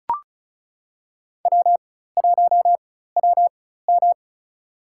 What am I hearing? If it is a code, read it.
Morse code, heard as a keyed tone, about 36 wpm, spaced as for 16 wpm, 700 Hz. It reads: W1WM
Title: USA Call Signs - 4 Characters - 4x Character Spacing - 35wpm